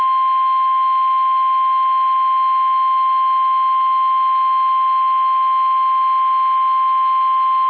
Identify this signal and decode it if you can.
Emergency Alert System (SAME) tones (the 1050 Hz attention tone). It is heard from 1.1 to 6.7 s.
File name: eas / KUSO / KUSO